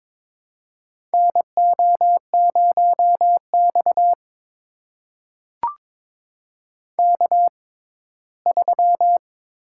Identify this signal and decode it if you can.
Morse code, heard as a keyed tone, about 22 wpm, spaced as for 15 wpm, 700 Hz. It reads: NO0X K3
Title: USA Call Signs - 4 Characters - 6x Character Spacing - 22wpm